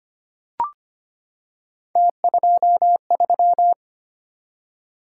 Morse code, heard as a keyed tone, about 25 wpm, 700 Hz. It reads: T23
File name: Three Letter-Number Combinations - 25wpm